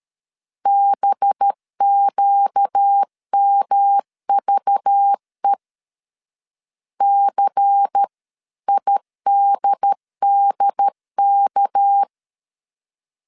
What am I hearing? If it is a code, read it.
Morse code, heard as a keyed tone, about 13 wpm, 800 Hz. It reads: BQMVE CIDDK
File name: CW / DET / 1 6 1500C S A 12wpm 3440